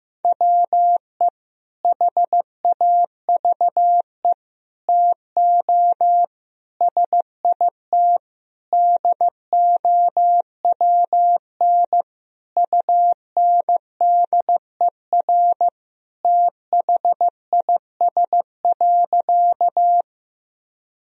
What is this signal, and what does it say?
Morse code, heard as a keyed tone, about 15 wpm, 700 Hz. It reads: WE HAVE TO SIT DOWN UNDER THIS.
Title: Sentences from Top 1000 Words - Review Part 1 - 15wpm